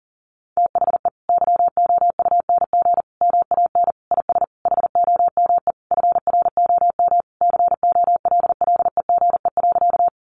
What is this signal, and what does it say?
Morse code, heard as a keyed tone, about 40 wpm, 700 Hz. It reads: THE YOUNG MAN IS HOME FROM COLLEGE.
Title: Sentences from Top 1300 Words 40wpm